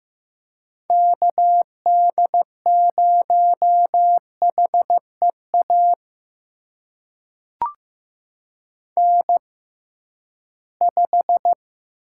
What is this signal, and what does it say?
Morse code, heard as a keyed tone, about 15 wpm, spaced as for 11 wpm, 700 Hz. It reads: KD0HEA N5
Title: USA Call Signs - 6x Character Spacing - 15wpm